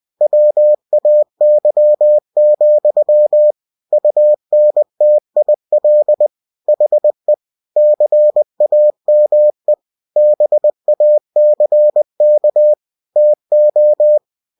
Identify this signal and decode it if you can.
Morse code, heard as a keyed tone, about 20 wpm, 600 Hz. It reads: WAY, UNTIL HE CAME BACK TO
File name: cw-books / winnie-the-pooh / winnie-20wpm-page-0027